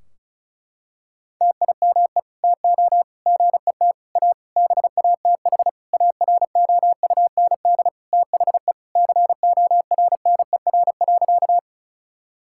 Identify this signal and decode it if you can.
Morse code, heard as a keyed tone, about 34 wpm, 700 Hz. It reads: TIME TO GET A BATH AROUND THE CORNER.
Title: Sentences from Top 1600 Words 35wpm